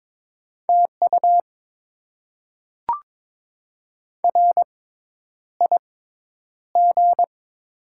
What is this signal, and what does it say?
Morse code, heard as a keyed tone, about 22 wpm, spaced as for 10 wpm, 700 Hz. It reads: TU RIG
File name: Top 20 QSO Elements - 6x Character Spacing - 22wpm